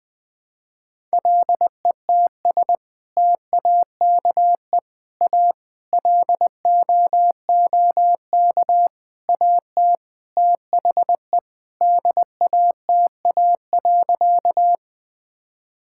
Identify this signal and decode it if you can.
Morse code, heard as a keyed tone, about 20 wpm, 700 Hz. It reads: LETS TAKE A LOOK AT THE DATA.
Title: Sentences with 6-Word Phrases - 20wpm